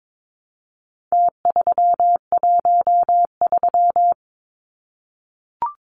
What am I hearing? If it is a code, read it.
Morse code, heard as a keyed tone, about 22 wpm, 700 Hz. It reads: T313
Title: Numbers - Contest Style - 4-Digit 22wpm